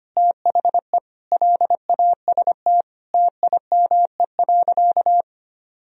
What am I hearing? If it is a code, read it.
Morse code, heard as a keyed tone, about 25 wpm, 700 Hz. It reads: THE LAST TIME.
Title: Sentences from Top 1000 Words - Review Part 2 - 25wpm